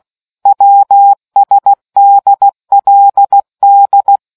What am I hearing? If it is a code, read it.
Morse code, heard as a keyed tone, about 16 wpm, 800 Hz. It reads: WSDLD